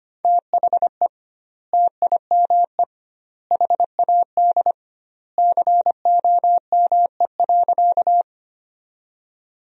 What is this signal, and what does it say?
Morse code, heard as a keyed tone, about 25 wpm, 700 Hz. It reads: THE TIME HAD COME.
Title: 2x Word-Spacing - Sentences from Top 100 Words - 25wpm